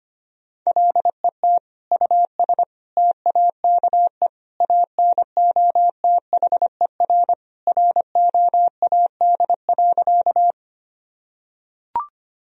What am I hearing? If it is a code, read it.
Morse code, heard as a keyed tone, about 25 wpm, 700 Hz. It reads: LET US TAKE ANOTHER ROAD.
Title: Sentences from Top 1000 Words - Review Part 2 - 25wpm